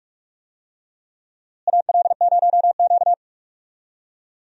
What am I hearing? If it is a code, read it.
Morse code, heard as a keyed tone, about 46 wpm, 700 Hz. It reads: AL0Q
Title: USA Call Signs - 4 Characters - Repeated 3x - 45wpm